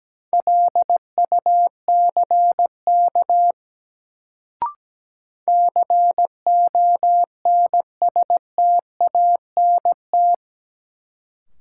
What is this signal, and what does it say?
Morse code, heard as a keyed tone, about 17 wpm, 700 Hz. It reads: LUCK CONSTANT
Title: Sentences from Top 1500 Words - New Word Drill - 17wpm